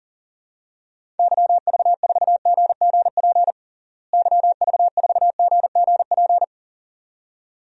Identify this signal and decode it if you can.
Morse code, heard as a keyed tone, about 40 wpm, 700 Hz. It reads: YV4GGP YV4GGP
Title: DXCC-100 Call Signs - 6 Characters - Repeated 2x - 40wpm